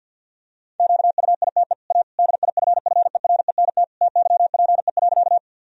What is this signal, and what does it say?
Morse code, heard as a keyed tone, about 50 wpm, 700 Hz. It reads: QUITE A DIFFERENT TYPE.
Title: Sentences from Top 1000 Words - Review Part 3 - 50wpm